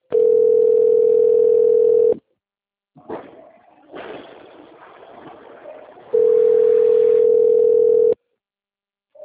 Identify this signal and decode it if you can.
Ringback tone (North America), 440 + 480 Hz.
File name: rg-1600-unknown- - - .39504